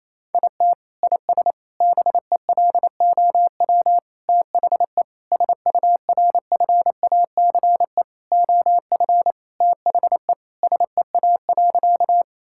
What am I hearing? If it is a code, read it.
Morse code, heard as a keyed tone, about 28 wpm, 700 Hz. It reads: IT IS BELOW THE SURFACE OF THE SEA.
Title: Sentences from Top 1000 Words - Review Part 5 - 28wpm